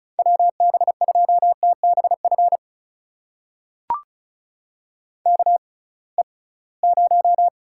Morse code, heard as a keyed tone, about 36 wpm, spaced as for 23 wpm, 700 Hz. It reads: WB2TBF KE0
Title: USA Call Signs - 6x Character Spacing - 35wpm